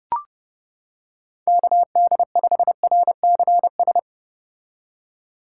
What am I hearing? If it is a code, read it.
Morse code, heard as a keyed tone, about 31 wpm, 700 Hz. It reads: KD5RCS